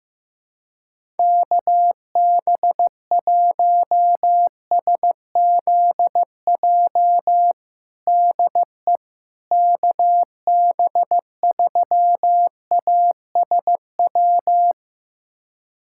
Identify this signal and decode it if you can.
Morse code, heard as a keyed tone, about 15 wpm, 700 Hz. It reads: KB1SZJ DE KB3ASW
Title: USA Callsign DE Callsign - 6 Characters - 15wpm